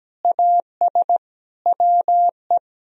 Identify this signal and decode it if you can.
Morse code, heard as a keyed tone, about 17 wpm, 700 Hz. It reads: AS WE
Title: Sentences from Top 1000 Words - Review Part 3 - 17wpm